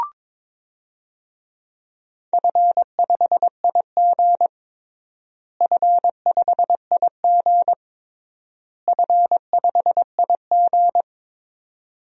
Morse code, heard as a keyed tone, about 22 wpm, 700 Hz. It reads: F5IG F5IG F5IG
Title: DXCC-100 Call Signs - 4 Characters - Repeated 3x - 22wpm